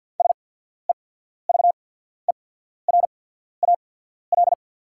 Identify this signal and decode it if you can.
Morse code, heard as a keyed tone, about 48 wpm, spaced as for 14 wpm, 690 Hz. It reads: SEVERAL
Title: 8x Character Spacing - Top 500 Words - 50wpm